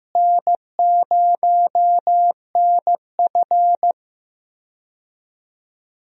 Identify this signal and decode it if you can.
Morse code, heard as a keyed tone, about 16 wpm, 700 Hz. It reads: N0NF
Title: USA Call Signs - 4 Characters - Repeated 3x - 15wpm